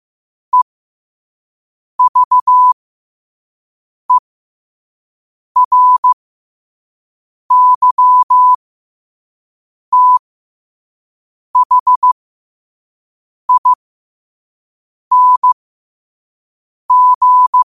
Morse code, heard as a keyed tone, about 15 wpm, spaced as for 5 wpm, 1000 Hz. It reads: EVERYTHING
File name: qotdfiles / 5wpm